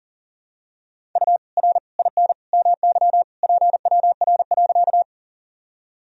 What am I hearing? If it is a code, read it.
Morse code, heard as a keyed tone, about 40 wpm, 700 Hz. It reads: U R IN MY PWR.
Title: Sentences from Top 500 Words and Abbreviations 40wpm